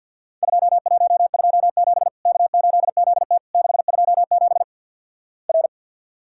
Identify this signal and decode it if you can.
Morse code, heard as a keyed tone, about 50 wpm, 700 Hz. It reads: 2127 K8ZT 627 R
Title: Worked All Europe DX Contest - QTC Exchange - 50wpm